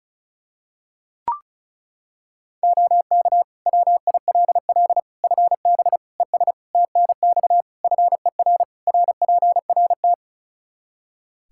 Morse code, heard as a keyed tone, about 34 wpm, 700 Hz. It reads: OK WILL FB ES TNX FER RPRT
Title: Practice QSO - Exchange 2 - Info - 35wpm